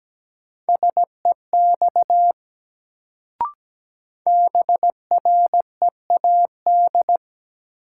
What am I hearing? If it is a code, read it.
Morse code, heard as a keyed tone, about 17 wpm, 700 Hz. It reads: SEX BREAD
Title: Sentences from Top 1600 Words - New Word Drill - 17wpm